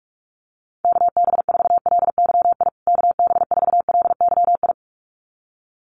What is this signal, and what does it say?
Morse code, heard as a keyed tone, about 44 wpm, 690 Hz. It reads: KB4LYI KB4LYI
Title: USA Call Signs Repeated 2x 45wpm